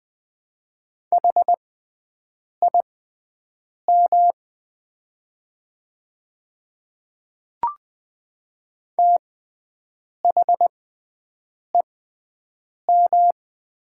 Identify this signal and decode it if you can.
Morse code, heard as a keyed tone, about 20 wpm, spaced as for 7 wpm, 700 Hz. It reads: HIM THEM
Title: 6x Character Spacing - Top 100 Words - 20wpm